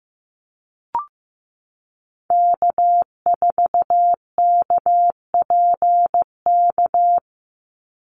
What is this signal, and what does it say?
Morse code, heard as a keyed tone, about 15 wpm, 700 Hz. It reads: K4KPK
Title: NA SOTA Activator Callsigns 15wpm